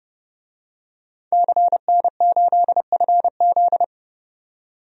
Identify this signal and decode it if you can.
Morse code, heard as a keyed tone, about 29 wpm, 700 Hz. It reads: CN8FZ
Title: DXCC-100 Call Signs - 5 Characters - 30wpm